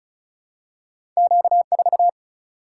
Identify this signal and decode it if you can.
Morse code, heard as a keyed tone, about 36 wpm, 700 Hz. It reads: Q 4
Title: Single-Letter Single-Number 35wpm